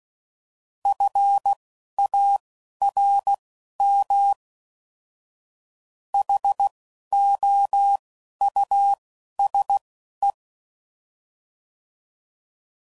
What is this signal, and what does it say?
Morse code, heard as a keyed tone, about 16 wpm, spaced as for 11 wpm, 800 Hz. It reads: FARM HOUSE